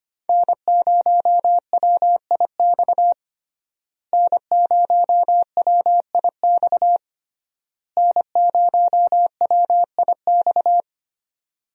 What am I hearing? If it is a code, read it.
Morse code, heard as a keyed tone, about 25 wpm, 700 Hz. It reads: N0WIX N0WIX N0WIX
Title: USA Call Signs - 5 Characters - Repeated 3x - 25wpm